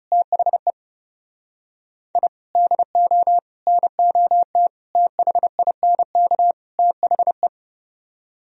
Morse code, heard as a keyed tone, about 40 wpm, 700 Hz. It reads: THE I DO NOT THINK THE